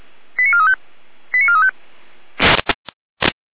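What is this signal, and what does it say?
Five-tone selective call (ZVEI), heard as quick runs of five tones, tones of about 70 ms. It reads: 89435 89435
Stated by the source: ZVEI five-tone selective call received via SDR.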